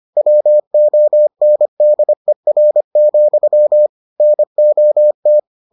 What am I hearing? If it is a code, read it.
Morse code, heard as a keyed tone, about 25 wpm, 600 Hz. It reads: WONDER, NOT